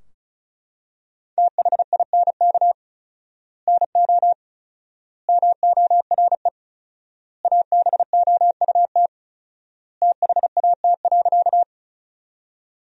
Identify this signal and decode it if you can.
Morse code, heard as a keyed tone, about 36 wpm, 700 Hz. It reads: THINK NO MORE ABOUT THAT.
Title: 4x Word-Spacing - Sentences from Top 200 Words - 35wpm